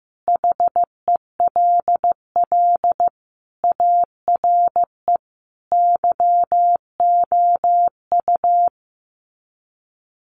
Morse code, heard as a keyed tone, about 14 wpm, 700 Hz. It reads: HELL ARE YOU
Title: Sets of 5 Words Encore 15wpm